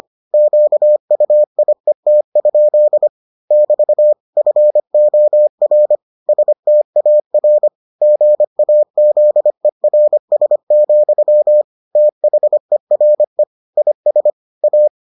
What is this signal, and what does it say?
Morse code, heard as a keyed tone, about 25 wpm, 600 Hz. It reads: QUIET? <BT> FOR STAR GAZERS, THERE IS A